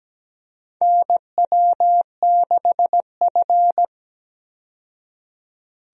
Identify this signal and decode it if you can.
Morse code, heard as a keyed tone, about 17 wpm, 700 Hz. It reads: NW6F
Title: USA Call Signs - 4 Characters - 17wpm